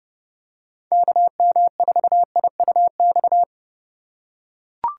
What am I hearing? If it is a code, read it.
Morse code, heard as a keyed tone, about 29 wpm, 700 Hz. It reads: KM4IUX